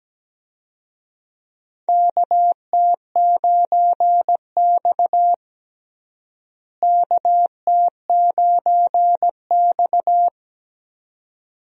Morse code, heard as a keyed tone, about 17 wpm, 700 Hz. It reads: KT9X KT9X
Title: USA Call Signs - 4 Characters - Repeated 2x - 17wpm